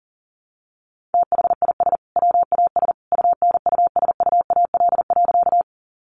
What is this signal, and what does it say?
Morse code, heard as a keyed tone, about 40 wpm, 700 Hz. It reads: THIS WAS UNUSUAL.
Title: Sentences from Top 1400 Words 40wpm